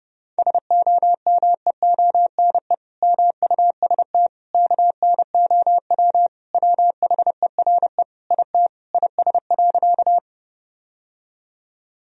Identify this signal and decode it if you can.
Morse code, heard as a keyed tone, about 30 wpm, 700 Hz. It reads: SOMEONE MUST KNOW WHERE IT IS.
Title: Sentences from Top 1000 Words - Review Part 4 - 30wpm